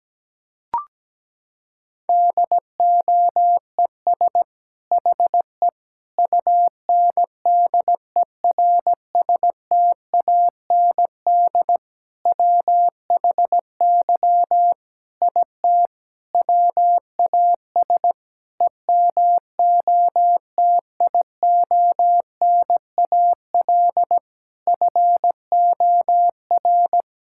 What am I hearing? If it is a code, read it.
Morse code, heard as a keyed tone, about 17 wpm, 700 Hz. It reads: DOES HE UNDERSTAND WHY IT WAS EMOTIONAL FOR
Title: Sentences from Top 1000 Words - Review Part 5 - 17wpm